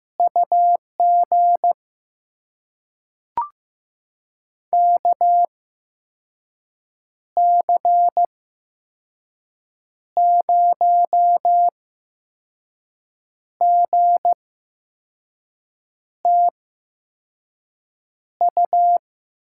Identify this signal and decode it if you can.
Morse code, heard as a keyed tone, about 15 wpm, spaced as for 5 wpm, 700 Hz. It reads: UG KC0GTU
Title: USA Call Signs - 6 Characters - 8x Character Spacing - 15wpm